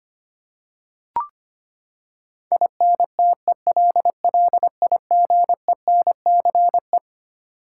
Morse code, heard as a keyed tone, about 25 wpm, 700 Hz. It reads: INTELLIGENCE
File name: Twelve Letter Words - 25wpm